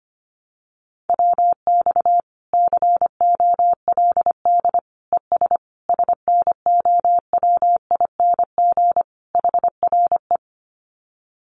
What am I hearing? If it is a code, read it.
Morse code, heard as a keyed tone, about 25 wpm, 700 Hz. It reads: WX COLD ES SNOWING HRE
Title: Common QSO Phrases 25wpm